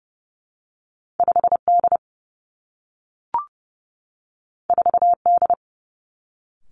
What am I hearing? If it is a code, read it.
Morse code, heard as a keyed tone, about 30 wpm, 700 Hz. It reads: 5 D 4 D
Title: Two Challenging Letter-Number Combinations 30wpm